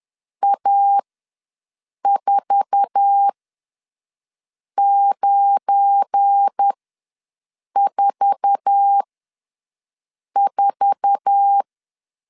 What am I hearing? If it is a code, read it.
Morse code, heard as a keyed tone, about 10 wpm, spaced as for 5 wpm, 800 Hz. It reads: A4944
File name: LEZIONI / 8 YJ45 / 8 YJ4 ws 10wpm 4020